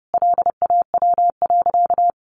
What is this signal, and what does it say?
Morse code, heard as a keyed tone, about 28 wpm, 700 Hz. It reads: LAW.
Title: Sentences from Top 1400 Words 30wpm